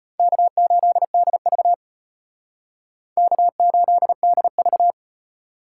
Morse code, heard as a keyed tone, about 32 wpm, 700 Hz. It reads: K8DV K8DV